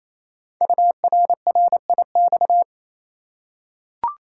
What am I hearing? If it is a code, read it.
Morse code, heard as a keyed tone, about 27 wpm, 700 Hz. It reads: URRIX
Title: Code Groups - 5 Letters - 28wpm